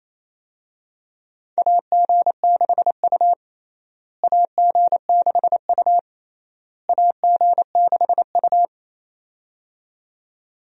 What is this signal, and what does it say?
Morse code, heard as a keyed tone, about 28 wpm, 700 Hz. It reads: AG6U AG6U AG6U
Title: USA Call Signs - 4 Characters - Repeated 3x - 28wpm